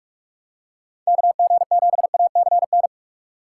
Morse code, heard as a keyed tone, about 46 wpm, 700 Hz. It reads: KG7ACN
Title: USA Call Signs - 6 Characters - 6x Character Spacing - 45wpm